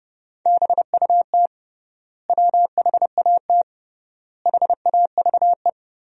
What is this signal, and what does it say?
Morse code, heard as a keyed tone, about 30 wpm, 700 Hz. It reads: BUT WHAT HAVE
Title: 3x Word-Spacing - Sentences from Top 100 Words - 30wpm